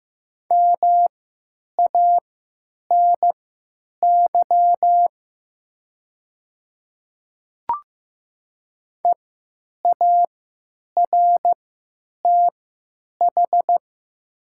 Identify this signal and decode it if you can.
Morse code, heard as a keyed tone, about 15 wpm, spaced as for 8 wpm, 700 Hz. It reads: MANY EARTH
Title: 3x Character Spacing - Top 200 Words - 15wpm